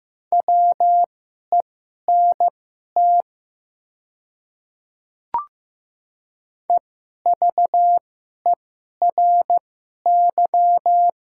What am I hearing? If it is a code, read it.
Morse code, heard as a keyed tone, about 15 wpm, spaced as for 11 wpm, 700 Hz. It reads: WENT EVERY
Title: 2x Character Spacing - Top 300 Words - 15wpm